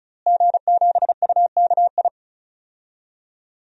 Morse code, heard as a keyed tone, about 36 wpm, 700 Hz. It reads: G7UKI
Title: DXCC-100 Call Signs - 5 Characters - Repeated 3x - 35wpm